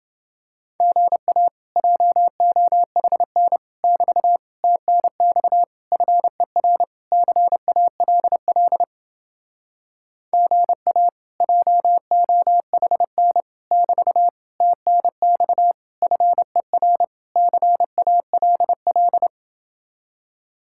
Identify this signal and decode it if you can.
Morse code, heard as a keyed tone, about 30 wpm, 700 Hz. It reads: GA JOHN <BT> TNX FER CALL GA JOHN <BT> TNX FER CALL